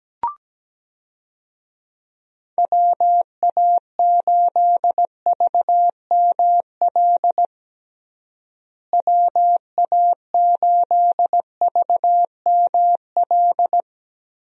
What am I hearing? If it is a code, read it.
Morse code, heard as a keyed tone, about 17 wpm, 700 Hz. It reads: WA8VML WA8VML
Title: USA Call Signs - 6 Characters - Repeated 3x - 17wpm